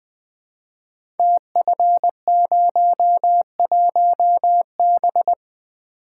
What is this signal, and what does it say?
Morse code, heard as a keyed tone, about 20 wpm, 700 Hz. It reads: TF01B
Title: Code Groups - 5 Letter:Numbers - 20wpm